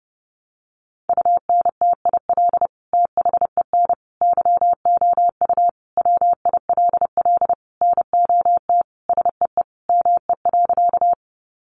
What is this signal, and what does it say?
Morse code, heard as a keyed tone, about 30 wpm, 700 Hz. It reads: UNTIL THEN YOU WILL NOT SEE ME.
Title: Sentences from Top 600 Words 30wpm